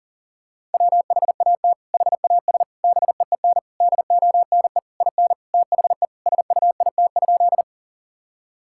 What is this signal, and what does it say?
Morse code, heard as a keyed tone, about 37 wpm, 700 Hz. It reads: WHAT HAS BEEN DONE IN THE SUIT?
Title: Sentences from Top 1600 Words 40wpm